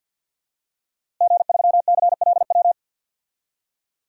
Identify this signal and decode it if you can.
Morse code, heard as a keyed tone, about 51 wpm, 700 Hz. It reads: G3CLW
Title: DXCC-100 Call Signs - 5 Characters - Repeated 3x - 50wpm